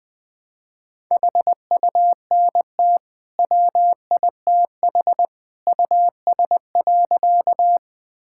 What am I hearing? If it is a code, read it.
Morse code, heard as a keyed tone, about 20 wpm, 700 Hz. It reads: HUNT WITH US.
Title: Sentences from Top 1600 Words 20wpm